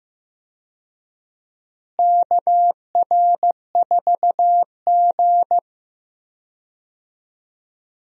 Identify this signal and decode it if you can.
Morse code, heard as a keyed tone, about 15 wpm, 700 Hz. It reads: KR4G
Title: USA Call Signs - 4 Characters - Repeated 3x - 15wpm